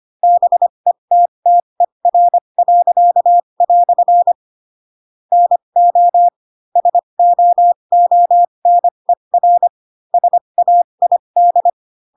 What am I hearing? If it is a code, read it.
Morse code, heard as a keyed tone, about 25 wpm, spaced as for 22 wpm, 700 Hz. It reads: BETTER." NO SOONER SAID